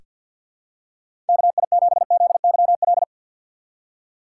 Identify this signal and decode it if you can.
Morse code, heard as a keyed tone, about 47 wpm, 700 Hz. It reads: KI7ZYL
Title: USA Call Signs - 2x Character Spacing - 50wpm